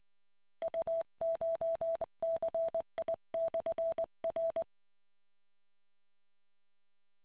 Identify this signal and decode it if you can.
Morse code, heard as a keyed tone, about 20 wpm, 660 Hz. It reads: UCHCI/R